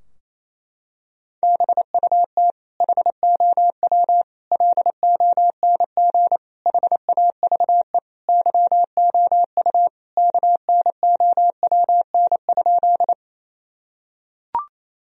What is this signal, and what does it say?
Morse code, heard as a keyed tone, about 28 wpm, 700 Hz. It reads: BUT HOW LONG HAVE YOU KNOWN?
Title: Sentences from Top 1000 Words - Review Part 3 - 28wpm